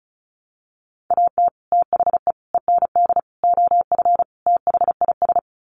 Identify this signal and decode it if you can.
Morse code, heard as a keyed tone, about 34 wpm, 700 Hz. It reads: AT THE END OF THIS
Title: Sets of 5 Words 35wpm